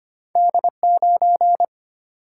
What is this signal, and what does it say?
Morse code, heard as a keyed tone, about 25 wpm, 700 Hz. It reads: D 9
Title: Single-Letter Single-Number 25wpm